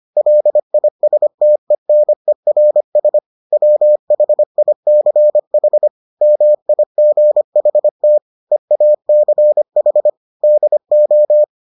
Morse code, heard as a keyed tone, about 25 wpm, 600 Hz. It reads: LISTENERS WHICH MIGHT EACH DO